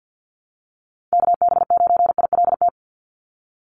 Morse code, heard as a keyed tone, about 50 wpm, 700 Hz. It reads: KB9ILT
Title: NA SOTA Activator Callsigns 50wpm